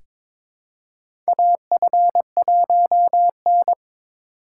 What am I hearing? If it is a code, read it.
Morse code, heard as a keyed tone, about 22 wpm, 700 Hz. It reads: AF1N
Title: USA Call Signs - 4 Characters - 3x Character Spacing - 22wpm